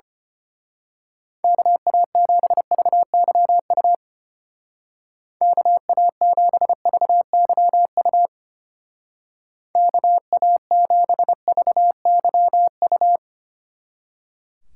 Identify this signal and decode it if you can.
Morse code, heard as a keyed tone, about 29 wpm, 700 Hz. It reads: KA7VYU KA7VYU KA7VYU
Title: Speed Racing - USA 2x3 Call Signs 25wpm